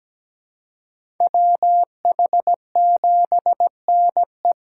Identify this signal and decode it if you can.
Morse code, heard as a keyed tone, about 17 wpm, 700 Hz. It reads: WH7NE
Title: USA Call Signs - 5 Characters - 3x Character Spacing - 17wpm